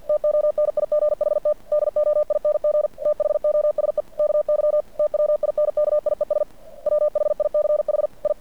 Morse code, heard as a keyed tone, about 50 wpm, 590 Hz. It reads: TONIGHT DOING THOSE KY TWINKIES WHICH I
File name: archived / CRSnet